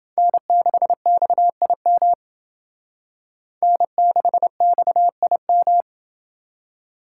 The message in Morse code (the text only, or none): N6XIM N6XIM